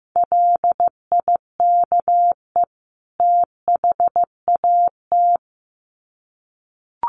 Morse code, heard as a keyed tone, about 15 wpm, 700 Hz. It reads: LIKE THAT
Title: Sets of 2 Words 15wpm